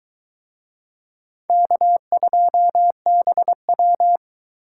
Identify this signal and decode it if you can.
Morse code, heard as a keyed tone, about 21 wpm, 700 Hz. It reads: K2BW